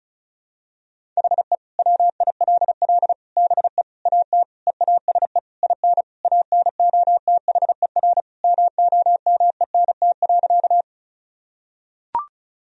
Morse code, heard as a keyed tone, about 36 wpm, 700 Hz. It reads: HE WILL BE AT EASE IN ANOTHER MOMENT.
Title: Sentences from Top 1000 Words - Review Part 1 - 35wpm